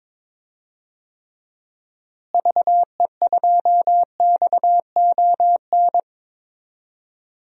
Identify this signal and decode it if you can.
Morse code, heard as a keyed tone, about 22 wpm, 700 Hz. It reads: VE2XON
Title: DXCC-100 Call Signs - 6 Characters - Repeated 2x - 22wpm